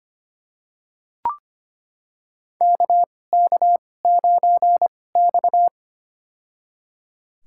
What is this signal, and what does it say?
Morse code, heard as a keyed tone, about 25 wpm, spaced as for 18 wpm, 700 Hz. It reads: KK9X
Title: USA Call Signs - 2x Character Spacing - 25wpm